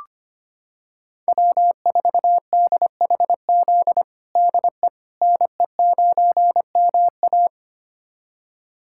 Morse code, heard as a keyed tone, about 25 wpm, 700 Hz. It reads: W4DHZ DE NE9MA
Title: USA Callsign DE Callsign - 5 Characters - 25wpm